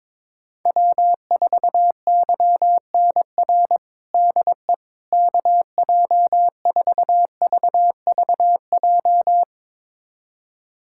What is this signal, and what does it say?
Morse code, heard as a keyed tone, about 22 wpm, 700 Hz. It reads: W4YNR DE KJ4VVJ
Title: USA Callsign DE Callsign 22wpm